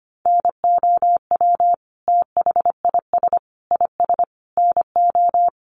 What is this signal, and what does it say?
Morse code, heard as a keyed tone, about 25 wpm, 700 Hz. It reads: NOW THIS IS NO
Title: Sentences from Top 1400 Words 25wpm